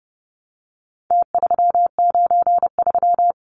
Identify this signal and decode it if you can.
Morse code, heard as a keyed tone, about 30 wpm, 700 Hz. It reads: T393
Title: Numbers - Contest Style - 4-Digit 30wpm